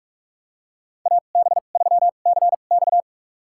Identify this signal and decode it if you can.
Morse code, heard as a keyed tone, about 45 wpm, spaced as for 32 wpm, 700 Hz. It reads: AB3CX
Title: USA Call Signs - 5 Characters - 2x Character Spacing - 45wpm